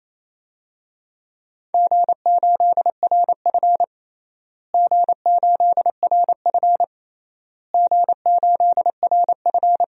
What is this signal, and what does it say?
Morse code, heard as a keyed tone, about 28 wpm, 700 Hz. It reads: G8RF G8RF G8RF
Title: DXCC-100 Call Signs - 4 Characters - Repeated 3x - 28wpm